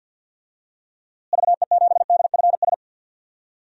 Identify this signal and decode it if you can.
Morse code, heard as a keyed tone, about 52 wpm, 700 Hz. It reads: VE7DFS
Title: DXCC-100 Call Signs - 6 Characters - Repeated 3x - 50wpm